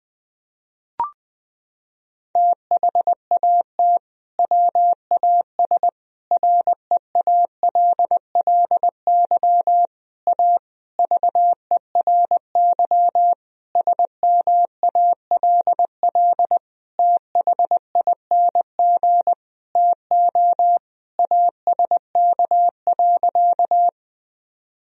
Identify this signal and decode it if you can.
Morse code, heard as a keyed tone, about 21 wpm, 700 Hz. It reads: THAT WAS REALLY A VERY SMALL THING TO ASK.
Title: Sentences from Top 1000 Words - Review Part 1 - 20wpm